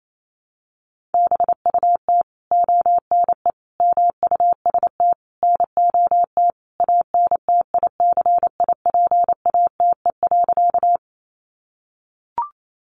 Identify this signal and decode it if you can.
Morse code, heard as a keyed tone, about 28 wpm, 700 Hz. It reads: BUT ONE MUST NOT ANTICIPATE.
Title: Sentences from Top 1500 Words 28wpm